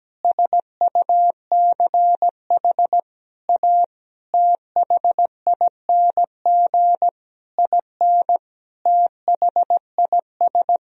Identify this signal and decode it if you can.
Morse code, heard as a keyed tone, about 17 wpm, 700 Hz. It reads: SUCH A THING IN THIS
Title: Sentences from Top 1000 Words - Review Part 3 - 17wpm